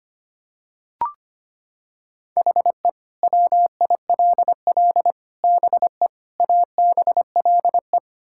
Morse code, heard as a keyed tone, about 25 wpm, 700 Hz. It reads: HE WILL BE ABLE
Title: Sentences from Top 1600 Words 25wpm